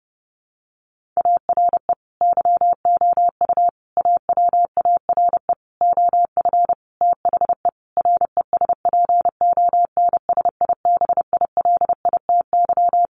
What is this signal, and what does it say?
Morse code, heard as a keyed tone, about 30 wpm, 700 Hz. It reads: ARE YOU AWARE OF THE RESPONSIBILITY
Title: Sentences from Top 1500 Words 30wpm